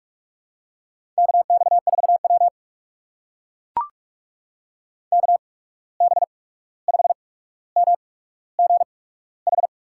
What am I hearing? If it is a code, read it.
Morse code, heard as a keyed tone, about 44 wpm, spaced as for 17 wpm, 700 Hz. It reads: KX4W KB5MGH
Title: USA Call Signs - 8x Character Spacing - 45wpm